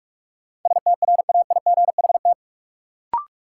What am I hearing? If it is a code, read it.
Morse code, heard as a keyed tone, about 44 wpm, 700 Hz. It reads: STRAIGHT
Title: Sentences from Top 1000 Words - New Word Drill - 45wpm